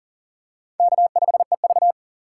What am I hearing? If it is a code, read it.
Morse code, heard as a keyed tone, about 40 wpm, 700 Hz. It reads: K5EV
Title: USA Call Signs - 4 Characters - 8x Character Spacing - 40wpm